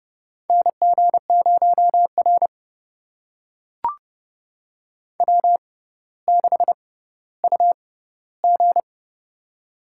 Morse code, heard as a keyed tone, about 30 wpm, spaced as for 15 wpm, 700 Hz. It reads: NG0R W6UG
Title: USA Call Signs - 6x Character Spacing - 30wpm